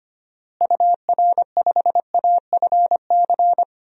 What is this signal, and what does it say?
Morse code, heard as a keyed tone, about 25 wpm, 700 Hz. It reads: UR5AFC
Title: DXCC-100 Call Signs - 6 Characters - Repeated 2x - 25wpm